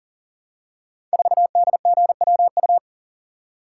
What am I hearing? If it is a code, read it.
Morse code, heard as a keyed tone, about 43 wpm, 700 Hz. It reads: 4DGWU
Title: Code Groups - 5 Letter:Numbers - 40wpm